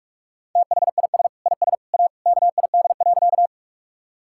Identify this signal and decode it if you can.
Morse code, heard as a keyed tone, about 46 wpm, 700 Hz. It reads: THIS IS A KID.
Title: Sentences from Top 1600 Words 45wpm